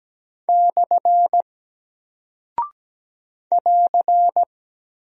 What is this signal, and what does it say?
Morse code, heard as a keyed tone, about 17 wpm, 700 Hz. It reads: / <AR>
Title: Single Character (Letter:Number:Punctuation:Prosign) 17wpm